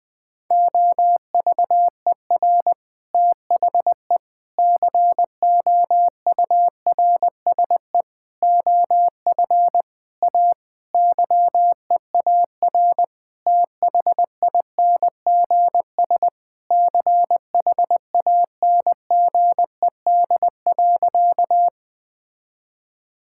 Morse code, heard as a keyed tone, about 20 wpm, 700 Hz. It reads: OVER THE COURSE OF A YEAR THINGS CHANGED.
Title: Sentences with 4-Word Phrases - Spoken Beginning - 20wpm